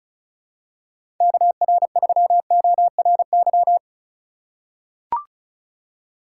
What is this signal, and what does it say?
Morse code, heard as a keyed tone, about 36 wpm, 700 Hz. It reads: KR3ORY
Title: USA Call Signs - 4x Character Spacing - 35wpm